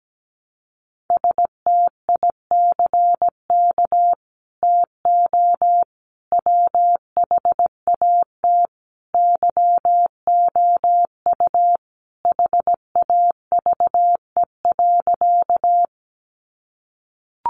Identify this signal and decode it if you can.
Morse code, heard as a keyed tone, about 17 wpm, 700 Hz. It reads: STICK TO WHAT YOU HAVE.
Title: Sentences from Top 1200 Words 17wpm